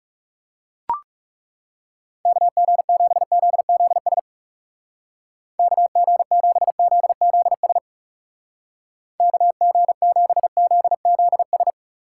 Morse code, heard as a keyed tone, about 38 wpm, 700 Hz. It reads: KG7ZZS KG7ZZS KG7ZZS